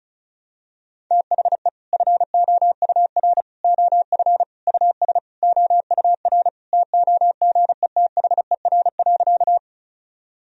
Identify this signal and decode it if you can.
Morse code, heard as a keyed tone, about 34 wpm, 700 Hz. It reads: THE FOUR OF US OUR TOGETHER.
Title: Sentences from Top 1000 Words - Review Part 1 - 35wpm